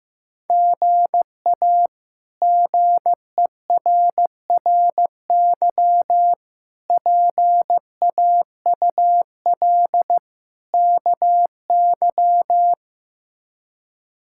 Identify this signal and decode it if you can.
Morse code, heard as a keyed tone, about 15 wpm, 700 Hz. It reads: GA GERRY PAUL KY
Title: K1USN SST QSO Practice - 15wpm